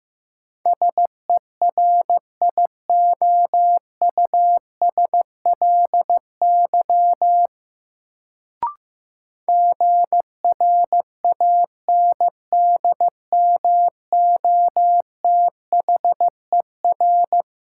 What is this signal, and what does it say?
Morse code, heard as a keyed tone, about 15 wpm, 700 Hz. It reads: SERIOUSLY GRANDMOTHER
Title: Sentences from Top 1200 Words - New Word Drill - 15wpm